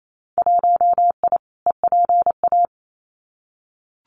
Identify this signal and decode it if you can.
Morse code, heard as a keyed tone, about 28 wpm, 700 Hz. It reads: 1I EPA
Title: Winter Field Day Exchanges Encore 28wpm